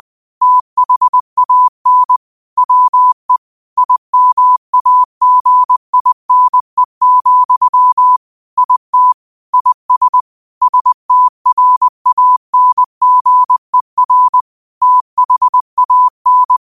Morse code, heard as a keyed tone, about 20 wpm, 1000 Hz. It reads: THAN WE IMAGINE, IT IS STRANGER THAN
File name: qotdfiles / 20wpm